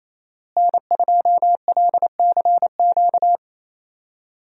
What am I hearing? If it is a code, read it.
Morse code, heard as a keyed tone, about 28 wpm, 700 Hz. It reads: N2LCQ
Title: USA Call Signs - 5 Characters - Repeated 2x - 28wpm